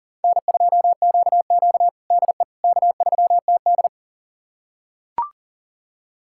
Morse code, heard as a keyed tone, about 40 wpm, 700 Hz. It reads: N2QQ DE K3TD
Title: USA Callsign DE Callsign - 4 Characters - 40wpm